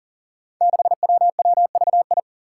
Morse code, heard as a keyed tone, about 43 wpm, 700 Hz. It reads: 6WWVI
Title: Five Challenging Letter-Number Combinations 40wpm